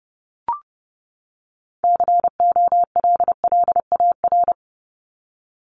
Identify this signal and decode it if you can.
Morse code, heard as a keyed tone, about 30 wpm, 700 Hz. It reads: COLLAR
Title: Top 1751-2000 Words 30wpm